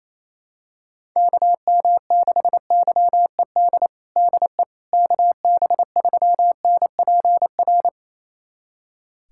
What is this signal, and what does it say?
Morse code, heard as a keyed tone, about 28 wpm, 700 Hz. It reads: KM6YED DE KB3NPR
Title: USA Callsign DE Callsign - 6 Characters - 28wpm